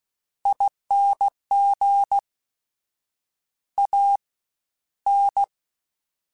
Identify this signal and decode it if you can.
Morse code, heard as a keyed tone, about 16 wpm, spaced as for 11 wpm, 800 Hz. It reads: ING AN
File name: Special INGs-mp3s / Special INGs-mp3s / Fast ing @15